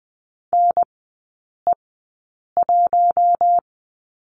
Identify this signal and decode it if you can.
Morse code, heard as a keyed tone, about 20 wpm, spaced as for 8 wpm, 700 Hz. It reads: NE1
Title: 041 Lesson 8e - Number 1 - Three Characters 20wpm